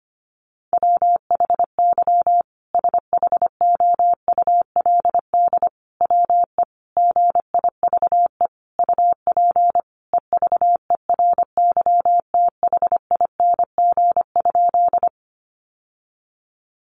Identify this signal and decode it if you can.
Morse code, heard as a keyed tone, about 25 wpm, 700 Hz. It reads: WHY SHOULD WE GIVE UP EVERYTHING?
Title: Sentences from Top 700 Words 25wpm